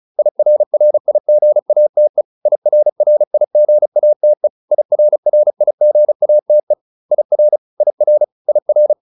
Morse code, heard as a keyed tone, about 35 wpm, 600 Hz. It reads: IRRIGATE IRRIGATE IRRIGATE IR IR IR
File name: IR8 35